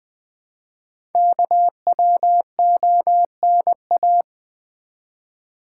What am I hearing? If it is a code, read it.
Morse code, heard as a keyed tone, about 20 wpm, 700 Hz. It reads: KWONA
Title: Code Groups - 5 Letters - 20wpm